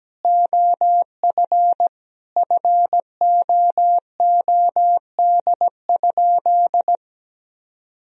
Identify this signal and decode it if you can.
Morse code, heard as a keyed tone, about 17 wpm, 700 Hz. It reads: OF FOOD?
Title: Sentences from Top 1000 Words - Review Part 3 - 17wpm